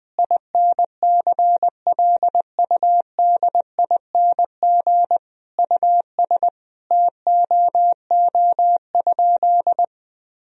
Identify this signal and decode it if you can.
Morse code, heard as a keyed tone, about 21 wpm, 700 Hz. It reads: INCLUDING US TOO?
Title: Sentences from Top 1000 Words - Review Part 4 - 20wpm